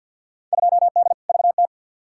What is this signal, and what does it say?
Morse code, heard as a keyed tone, about 51 wpm, 700 Hz. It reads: 2D VT
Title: Field Day Exchanges Encore - 50wpm